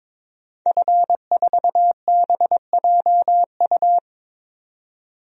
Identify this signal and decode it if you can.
Morse code, heard as a keyed tone, about 22 wpm, 700 Hz. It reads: F4BJU
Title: DXCC-100 Call Signs - 5 Characters - 22wpm